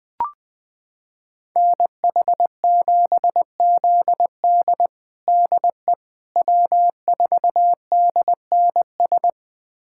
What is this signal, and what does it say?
Morse code, heard as a keyed tone, about 20 wpm, 700 Hz. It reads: NH7ZD DE W4DNS
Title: USA Callsign DE Callsign - 5 Characters - 20wpm